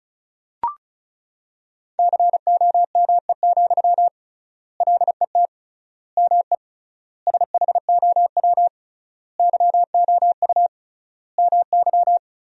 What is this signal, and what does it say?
Morse code, heard as a keyed tone, about 34 wpm, 700 Hz. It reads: COME, LET ME SHOW YOU MY
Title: 3x Word-Spacing - Sentences from Top 500 Words - 35wpm